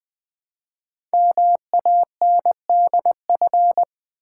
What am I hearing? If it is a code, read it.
Morse code, heard as a keyed tone, about 20 wpm, 700 Hz. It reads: MANDF
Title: Code Groups - 5 Letters - 20wpm